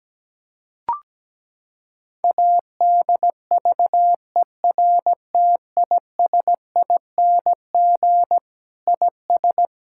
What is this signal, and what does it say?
Morse code, heard as a keyed tone, about 17 wpm, 700 Hz. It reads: ADVERTISING IS
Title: Sentences from Top 1000 Words - Review Part 6 - 17wpm